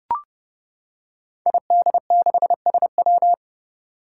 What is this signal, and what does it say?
Morse code, heard as a keyed tone, about 30 wpm, 700 Hz. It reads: ID6SW
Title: Five Challenging Letter-Number Combinations 30wpm